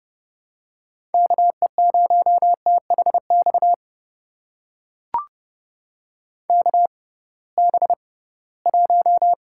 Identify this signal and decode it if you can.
Morse code, heard as a keyed tone, about 30 wpm, spaced as for 19 wpm, 700 Hz. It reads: KE0THX KB1
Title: USA Call Signs - 6 Characters - 6x Character Spacing - 30wpm